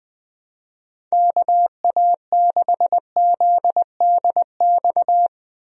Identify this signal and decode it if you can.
Morse code, heard as a keyed tone, about 20 wpm, 700 Hz. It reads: KA6ZDX
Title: USA Call Signs - 6 Characters - 20wpm